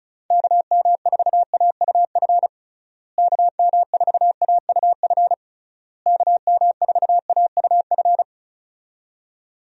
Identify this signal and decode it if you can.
Morse code, heard as a keyed tone, about 36 wpm, 700 Hz. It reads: KM4AUF KM4AUF KM4AUF
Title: USA Call Signs - 6 Characters - Repeated 3x - 35wpm